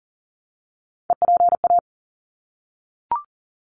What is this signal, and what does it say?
Morse code, heard as a keyed tone, about 40 wpm, 700 Hz. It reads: EPA
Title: ARRL Sections 40wpm